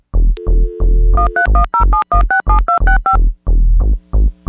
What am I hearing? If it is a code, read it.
DTMF, heard as telephone touch tones, tones of about 120 ms. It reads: 13107167265